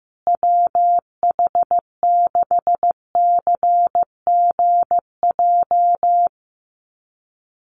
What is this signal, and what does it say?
Morse code, heard as a keyed tone, about 15 wpm, 700 Hz. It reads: WH6CGJ
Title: USA Call Signs Repeated 3x 15wpm